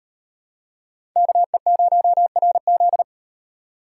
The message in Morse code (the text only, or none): KE0RZ